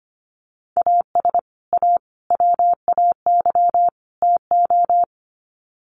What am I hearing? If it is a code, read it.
Morse code, heard as a keyed tone, about 25 wpm, 700 Hz. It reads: AS A WAY TO
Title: Sets of 4 Words 25wpm